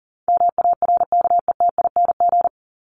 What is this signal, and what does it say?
Morse code, heard as a keyed tone, about 40 wpm, 700 Hz. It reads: MARKETING
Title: Top 751-1000 Words 40wpm